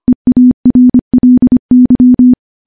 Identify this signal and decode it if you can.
Morse code, heard as a keyed tone, about 25 wpm, 250 Hz. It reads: EARLY
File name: morse-training-25WPM